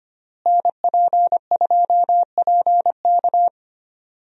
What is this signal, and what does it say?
Morse code, heard as a keyed tone, about 25 wpm, 700 Hz. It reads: NP2PK
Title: USA Call Signs - 5 Characters - Repeated 2x - 25wpm